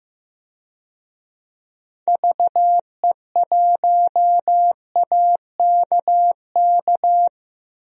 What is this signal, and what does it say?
Morse code, heard as a keyed tone, about 15 wpm, 700 Hz. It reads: VE1AKK
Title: DXCC-100 Call Signs - 6 Characters - 15wpm